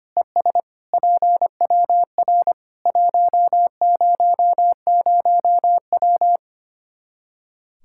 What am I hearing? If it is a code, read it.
Morse code, heard as a keyed tone, about 25 wpm, 700 Hz. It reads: ES PWR 100W
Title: K1USN SST QSO Practice 25wpm